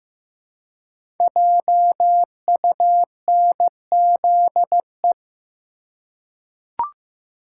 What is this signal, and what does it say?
Morse code, heard as a keyed tone, about 15 wpm, 700 Hz. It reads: JUNZE
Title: Code Groups - 5 Letters - 15wpm